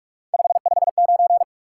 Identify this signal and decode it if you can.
Morse code, heard as a keyed tone, about 55 wpm, 700 Hz. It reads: 559
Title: Speed-Racing Top 180 QSO Elements 45wpm